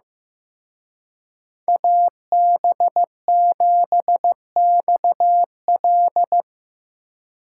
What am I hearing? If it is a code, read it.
Morse code, heard as a keyed tone, about 17 wpm, 700 Hz. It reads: AB7XL
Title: Speed Racing - USA 2x2 Call Signs 15wpm